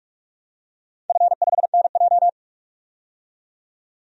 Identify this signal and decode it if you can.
Morse code, heard as a keyed tone, about 44 wpm, 690 Hz. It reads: F5NJ
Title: DXCC-100 Call Signs - 4 Characters - 45wpm